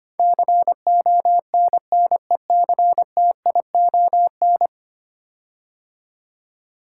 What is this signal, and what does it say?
Morse code, heard as a keyed tone, about 25 wpm, 700 Hz. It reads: CONNECTION
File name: 10 Letter Words - First 3 Letters Given - 25wpm